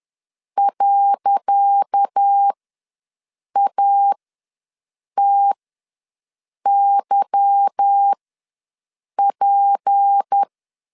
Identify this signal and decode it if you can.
Morse code, heard as a keyed tone, about 10 wpm, spaced as for 6 wpm, 800 Hz. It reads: .ATYP
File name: LEZIONI / 10 SIGNS / 10C DOT ws 10wpm 4213